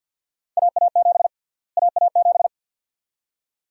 Morse code, heard as a keyed tone, about 52 wpm, 700 Hz. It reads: AA7 AA7
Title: USA Call Sign Prefixes - Repeated 2x - 50wpm